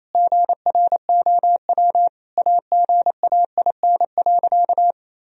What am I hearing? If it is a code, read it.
Morse code, heard as a keyed tone, about 28 wpm, 700 Hz. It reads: GROW AGAIN.
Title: Sentences from Top 1000 Words - Review Part 1 - 28wpm